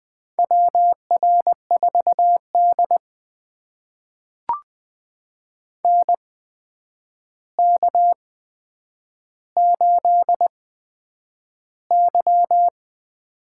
Morse code, heard as a keyed tone, about 20 wpm, spaced as for 9 wpm, 700 Hz. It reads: WR4D NK8Y
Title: USA Call Signs - 4 Characters - 8x Character Spacing - 20wpm